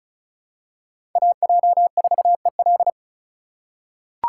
Morse code, heard as a keyed tone, about 34 wpm, 700 Hz. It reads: AJ4EL
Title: USA Call Signs - 5 Characters - 3x Character Spacing - 35wpm